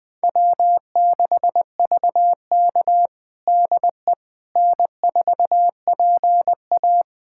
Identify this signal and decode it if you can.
Morse code, heard as a keyed tone, about 20 wpm, 700 Hz. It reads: W6VK DE N4PA
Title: USA Callsign DE Callsign - 4 Characters - 20wpm